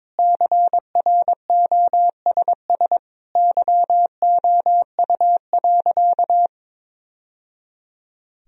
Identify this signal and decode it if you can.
Morse code, heard as a keyed tone, about 22 wpm, 700 Hz. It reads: CROSS YOU.
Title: Sentences from Top 1000 Words - Review Part 1 - 22wpm